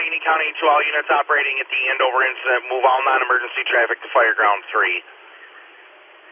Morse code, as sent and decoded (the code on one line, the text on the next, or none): none